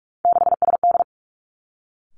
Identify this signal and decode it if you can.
Morse code, heard as a keyed tone, about 46 wpm, 700 Hz. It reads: 6SD
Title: Three Challenging Letter-Number Combinations 45wpm